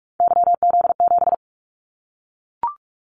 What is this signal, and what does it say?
Morse code, heard as a keyed tone, about 44 wpm, 700 Hz. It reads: YZ7
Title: Callsign Prefixes for Top 25 DXCC - Repeated 3x - 45wpm